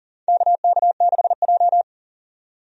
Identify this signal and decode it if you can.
Morse code, heard as a keyed tone, about 40 wpm, 700 Hz. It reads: KK6J
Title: USA Call Signs - 4 Characters - 40wpm